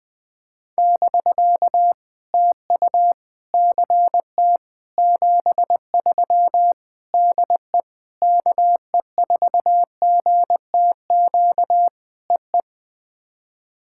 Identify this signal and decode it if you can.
Morse code, heard as a keyed tone, about 20 wpm, 700 Hz. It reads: <BK> TU CT 73 DE KE4GTQ EE
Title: POTA QSO Practice - 20wpm